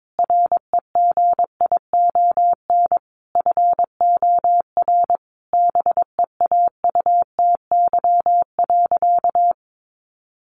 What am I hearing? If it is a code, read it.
Morse code, heard as a keyed tone, about 22 wpm, 700 Hz. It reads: REGION FOR BEAUTY.
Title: Sentences from Top 1500 Words 22wpm